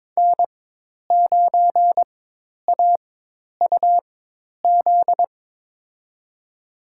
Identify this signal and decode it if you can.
Morse code, heard as a keyed tone, about 22 wpm, spaced as for 10 wpm, 700 Hz. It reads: N9AUZ
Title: USA Call Signs - 4x Character Spacing - 22wpm